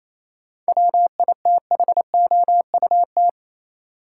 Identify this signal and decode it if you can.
Morse code, heard as a keyed tone, about 28 wpm, 700 Hz. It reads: WITHOUT
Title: Sentences from Top 600 Words - New Word Drill - 28wpm